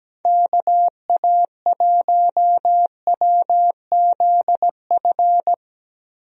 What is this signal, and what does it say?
Morse code, heard as a keyed tone, about 17 wpm, 700 Hz. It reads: KA1WZF
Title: USA Call Signs - 6 Characters - Repeated 3x - 17wpm